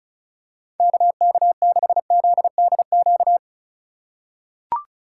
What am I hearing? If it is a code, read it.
Morse code, heard as a keyed tone, about 34 wpm, 700 Hz. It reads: KK6ZDQ
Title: USA Call Signs - 6 Characters - 2x Character Spacing - 35wpm